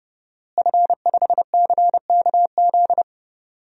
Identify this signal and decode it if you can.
Morse code, heard as a keyed tone, about 30 wpm, 700 Hz. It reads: F5CKZ
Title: DXCC-100 Call Signs - 5 Characters - Repeated 2x - 30wpm